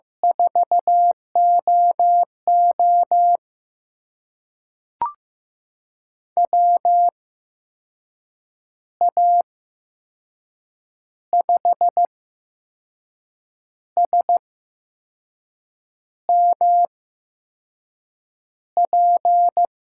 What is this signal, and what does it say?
Morse code, heard as a keyed tone, about 15 wpm, spaced as for 5 wpm, 700 Hz. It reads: 4OO WA5SMP
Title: USA Call Signs - 8x Character Spacing - 15wpm